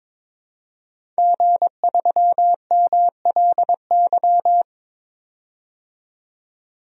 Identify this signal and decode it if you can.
Morse code, heard as a keyed tone, about 22 wpm, 700 Hz. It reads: G3MLY
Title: DXCC-100 Call Signs - 5 Characters - 22wpm